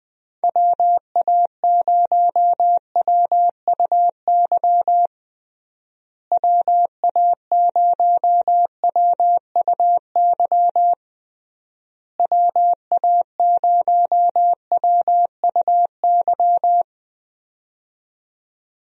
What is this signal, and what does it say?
Morse code, heard as a keyed tone, about 21 wpm, 700 Hz. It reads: WA0WUY WA0WUY WA0WUY
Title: USA Call Signs - 6 Characters - Repeated 3x - 20wpm